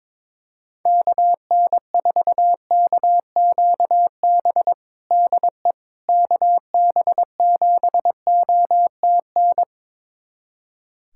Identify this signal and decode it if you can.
Morse code, heard as a keyed tone, about 22 wpm, 700 Hz. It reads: KN4KQB DE KB7OTN
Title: USA Callsign DE Callsign 22wpm